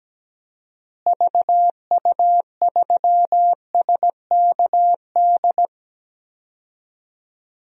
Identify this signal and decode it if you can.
Morse code, heard as a keyed tone, about 17 wpm, 700 Hz. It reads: VU3SKD
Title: DXCC-100 Call Signs - 6 Characters - Repeated 2x - 17wpm